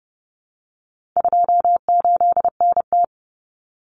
Morse code, heard as a keyed tone, about 30 wpm, 700 Hz. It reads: 28NT
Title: Common Cut Numbers - 4-Digit 30wpm